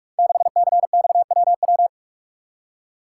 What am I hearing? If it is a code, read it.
Morse code, heard as a keyed tone, about 44 wpm, 700 Hz. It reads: 6CXWW
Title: Code Groups - 5 Letter:Numbers - 45wpm